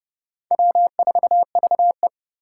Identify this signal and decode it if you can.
Morse code, heard as a keyed tone, about 30 wpm, 700 Hz. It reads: W4VE